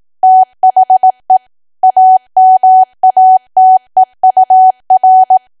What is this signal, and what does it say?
Morse code, heard as a keyed tone, about 18 wpm, 750 Hz. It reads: THE AMATEUR
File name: cwa / W1AW-Level2 / 18WPM